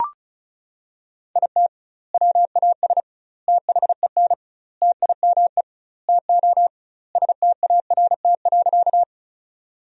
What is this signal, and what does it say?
Morse code, heard as a keyed tone, about 34 wpm, 700 Hz. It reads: IT WAS THEN TIME TO START.
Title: 2x Word-Spacing - Sentences from Top 300 Words - 35wpm